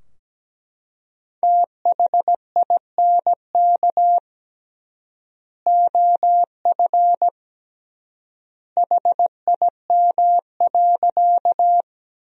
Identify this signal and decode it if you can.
Morse code, heard as a keyed tone, about 17 wpm, 700 Hz. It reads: THINK OF HIM.
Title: 3x Word-Spacing - Sentences from Top 200 Words - 17wpm